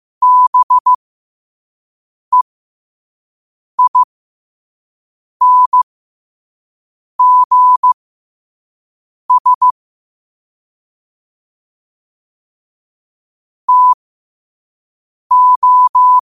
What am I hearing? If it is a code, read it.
Morse code, heard as a keyed tone, about 15 wpm, spaced as for 5 wpm, 1000 Hz. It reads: BEINGS TO
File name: 5wpm